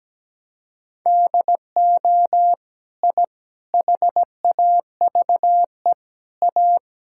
Morse code, heard as a keyed tone, about 17 wpm, 700 Hz. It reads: DO I HAVE A
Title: Sentences from Top 1000 Words - Review Part 6 - 17wpm